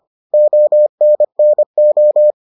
Morse code, heard as a keyed tone, about 25 wpm, 600 Hz. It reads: ONNO